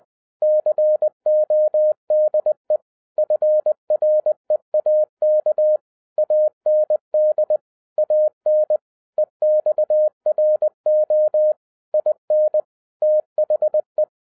Morse code, heard as a keyed tone, about 20 wpm, 610 Hz. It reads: CODE FREAK AND AN EXRO IN THE
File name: 2011-07-mp3 / n08 A LOT HAS CHANGED 20wpm